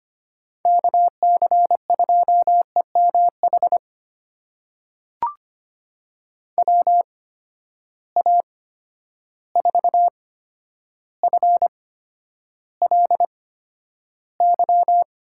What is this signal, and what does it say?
Morse code, heard as a keyed tone, about 25 wpm, spaced as for 11 wpm, 700 Hz. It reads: KC2EMH WA4FLY
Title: USA Call Signs - 6 Characters - 8x Character Spacing - 25wpm